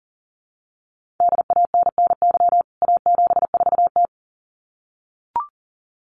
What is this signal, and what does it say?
Morse code, heard as a keyed tone, about 40 wpm, 700 Hz. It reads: DANNY A74T
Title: CWT Exchange 40wpm